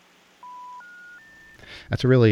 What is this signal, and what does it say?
Special information tone, with speech.